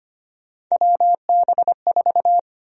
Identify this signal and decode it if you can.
Morse code, heard as a keyed tone, about 25 wpm, 700 Hz. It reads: WB4
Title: USA Call Sign Prefixes - 25wpm